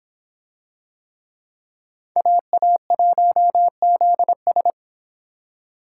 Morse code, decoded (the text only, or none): AA1ZS